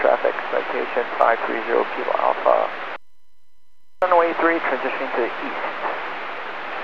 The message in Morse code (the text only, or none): none